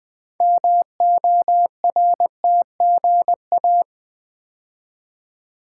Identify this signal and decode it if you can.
Morse code, heard as a keyed tone, about 20 wpm, 700 Hz. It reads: MORTGA
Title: Build A Word - 3 Letters At A Time - 8-Letter Words - 20wpm